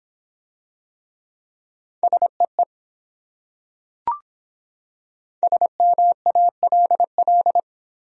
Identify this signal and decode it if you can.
Morse code, heard as a keyed tone, about 24 wpm, 700 Hz. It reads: SEE SMALL